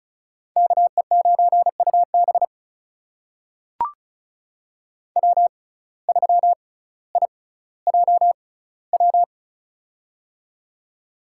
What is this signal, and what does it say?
Morse code, heard as a keyed tone, about 36 wpm, spaced as for 18 wpm, 700 Hz. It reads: KE9UB W3IJW
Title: USA Call Signs - 5 Characters - 6x Character Spacing - 35wpm